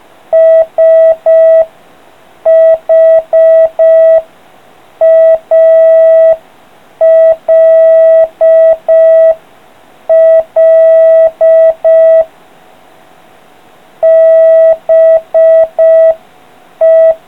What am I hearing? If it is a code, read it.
Morse code, heard as a keyed tone, about 5 wpm, 650 Hz. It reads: SHALL BE